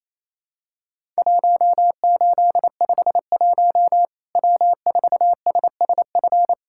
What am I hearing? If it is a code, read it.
Morse code, heard as a keyed tone, about 27 wpm, 700 Hz. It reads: 1851 W4SSF
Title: Worked All Europe DX Contest - QTC Exchange - 28wpm